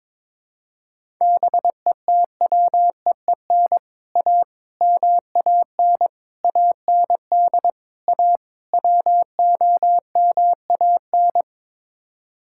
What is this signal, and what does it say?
Morse code, heard as a keyed tone, about 22 wpm, 700 Hz. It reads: BETWEEN A MAN AND A WOMAN
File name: practice / 22wpm / Sets of 6 Words - 22wpm